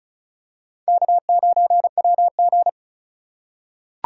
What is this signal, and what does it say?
Morse code, heard as a keyed tone, about 36 wpm, 700 Hz. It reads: K9WG
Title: USA Call Signs - 4 Characters - 2x Character Spacing - 35wpm